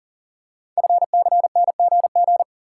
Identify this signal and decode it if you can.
Morse code, heard as a keyed tone, about 42 wpm, 700 Hz. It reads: FCNGG
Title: Code Groups - 5 Letters - 40wpm